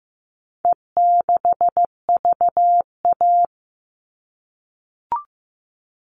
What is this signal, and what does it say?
Morse code, heard as a keyed tone, about 15 wpm, 700 Hz. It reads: E6VA
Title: All Cut Numbers - 4-Digit 15wpm